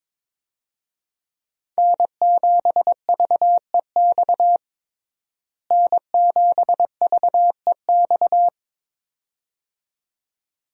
Morse code, heard as a keyed tone, about 22 wpm, 700 Hz. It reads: N7VEX N7VEX
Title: USA Call Signs - 5 Characters - Repeated 2x - 22wpm